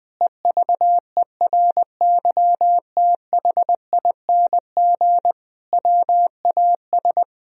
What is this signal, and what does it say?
Morse code, heard as a keyed tone, about 20 wpm, 700 Hz. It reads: EVERYTHING WAS
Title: Sentences from Top 1000 Words - Review Part 3 - 20wpm